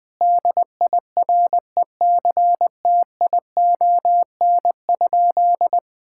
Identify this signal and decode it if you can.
Morse code, heard as a keyed tone, about 20 wpm, 700 Hz. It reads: DIRECTION?
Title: Sentences from Top 1000 Words - Review Part 5 - 20wpm